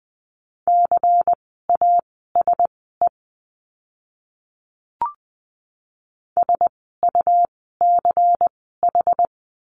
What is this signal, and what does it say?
Morse code, heard as a keyed tone, about 20 wpm, spaced as for 15 wpm, 700 Hz. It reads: CASE SUCH
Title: 115 Lesson 19f - Letter w - Words 20wpm